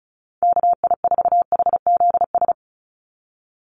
Morse code, heard as a keyed tone, about 34 wpm, 700 Hz. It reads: KI4HZS
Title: USA Call Signs 35wpm